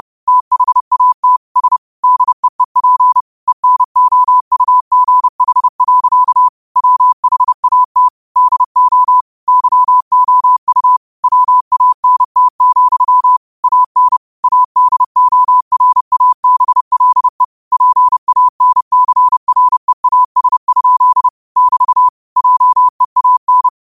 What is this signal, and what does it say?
Morse code, heard as a keyed tone, about 30 wpm, 1000 Hz. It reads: THAT S DEEP ENOUGH. WHAT DO YOU WANT, AN ADORABLE PANCREAS? <BT> JEAN